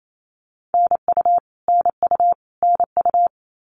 Morse code, heard as a keyed tone, about 28 wpm, 700 Hz. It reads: NU NU NU
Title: Canadian Provinces and Territories - Repeated 3x - 28wpm